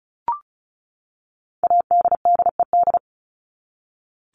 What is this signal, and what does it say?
Morse code, heard as a keyed tone, about 34 wpm, 700 Hz. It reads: ADDED
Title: ED Suffix - Top 100 Words with Base Words 35wpm